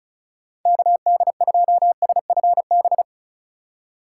Morse code, heard as a keyed tone, about 34 wpm, 700 Hz. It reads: KD2SFB
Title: USA Call Signs - 6 Characters - 3x Character Spacing - 35wpm